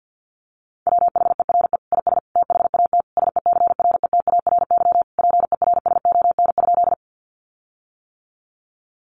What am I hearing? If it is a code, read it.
Morse code, heard as a keyed tone, about 50 wpm, 700 Hz. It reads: WHERE IS THAT SECRETARY PERSON?
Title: Sentences from Top 1300 Words 50wpm